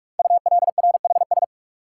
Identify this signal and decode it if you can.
Morse code, heard as a keyed tone, about 44 wpm, 700 Hz. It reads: ULRHS
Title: Code Groups - 5 Letters - 45wpm